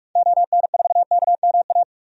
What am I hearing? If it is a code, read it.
Morse code, heard as a keyed tone, about 42 wpm, 700 Hz. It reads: ON4KMA
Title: DXCC-100 Call Signs - 6 Characters - 45wpm